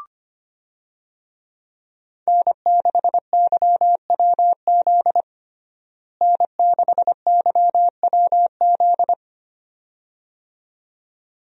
Morse code, heard as a keyed tone, about 25 wpm, 700 Hz. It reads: N6YWZ N6YWZ
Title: USA Call Signs - 5 Characters - Repeated 2x - 25wpm